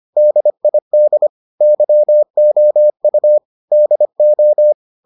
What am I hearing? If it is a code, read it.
Morse code, heard as a keyed tone, about 25 wpm, 600 Hz. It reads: DID YOU DO